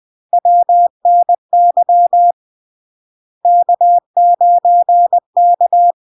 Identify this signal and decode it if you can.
Morse code, heard as a keyed tone, about 20 wpm, 700 Hz. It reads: WNY K9K
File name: LARK-FD-2022-Log-in-CW-20wpm-700-Hz